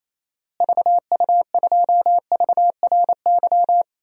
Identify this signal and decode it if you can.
Morse code, heard as a keyed tone, about 28 wpm, 700 Hz. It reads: VU2VRY
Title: DXCC-100 Call Signs - 6 Characters - Repeated 2x - 28wpm